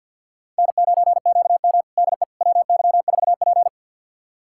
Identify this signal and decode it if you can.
Morse code, heard as a keyed tone, about 52 wpm, 700 Hz. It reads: N9QM DE WY4P
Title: USA Callsign DE Callsign - 4 Characters - 50wpm